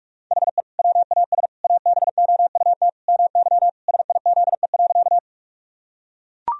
Morse code, heard as a keyed tone, about 42 wpm, 700 Hz. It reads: HE WAS ABOUT MY SIZE.
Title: Sentences from Top 1000 Words - Review Part 2 - 45wpm